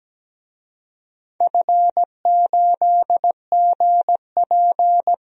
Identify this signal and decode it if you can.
Morse code, heard as a keyed tone, about 17 wpm, 700 Hz. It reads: F8GP
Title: DXCC-100 Call Signs - 4 Characters - Repeated 2x - 17wpm